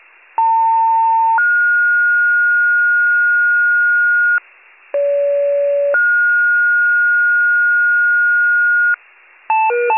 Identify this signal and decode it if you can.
Two-tone page, heard as a one-second tone, then a three-second tone, about 900 then 1430 Hz, about 570 then 1430 Hz. It is heard from 0.4 to 8.9 s.